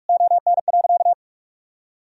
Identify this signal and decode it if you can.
Morse code, heard as a keyed tone, about 46 wpm, 700 Hz. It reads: ON.
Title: Sentences from Top 1000 Words - Review Part 1 - 45wpm